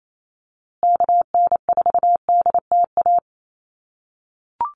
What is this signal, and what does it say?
Morse code, heard as a keyed tone, about 28 wpm, 700 Hz. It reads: KN4DTA
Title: USA Call Signs Repeated 2x 28wpm